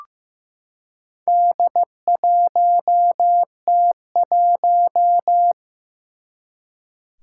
Morse code, heard as a keyed tone, about 15 wpm, 700 Hz. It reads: D1T1
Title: Four Challenging Letter-Number Combinations 15wpm